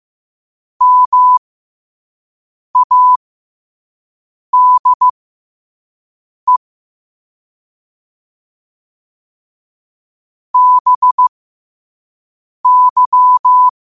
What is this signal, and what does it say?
Morse code, heard as a keyed tone, about 15 wpm, spaced as for 5 wpm, 1000 Hz. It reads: MADE BY